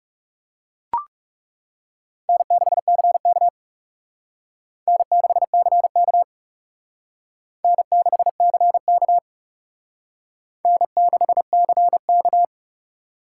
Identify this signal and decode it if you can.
Morse code, heard as a keyed tone, about 37 wpm, 700 Hz. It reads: N6CK N6CK N6CK N6CK
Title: Speed Racing - USA 1x2 Call Signs 30wpm